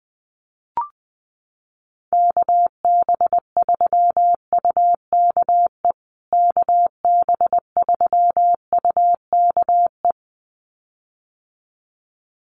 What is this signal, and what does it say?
Morse code, heard as a keyed tone, about 20 wpm, 700 Hz. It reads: KB3UKE KB3UKE
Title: USA Call Signs Repeated 2x 20wpm